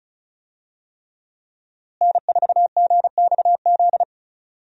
Morse code, heard as a keyed tone, about 36 wpm, 700 Hz. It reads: N4GXZ
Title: USA Call Signs - 5 Characters - Repeated 3x - 35wpm